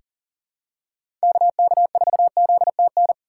Morse code, heard as a keyed tone, about 40 wpm, 700 Hz. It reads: KK4ZTN
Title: USA Call Signs - 4x Character Spacing - 40wpm